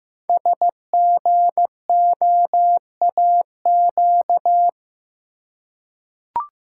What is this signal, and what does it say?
Morse code, heard as a keyed tone, about 15 wpm, 700 Hz. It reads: SGOAQ
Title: Code Groups - 5 Letters - 15wpm